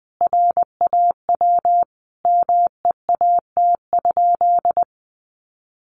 Morse code, heard as a keyed tone, about 20 wpm, 700 Hz. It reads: RAW MEAT?
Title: Sentences from Top 1500 Words 20wpm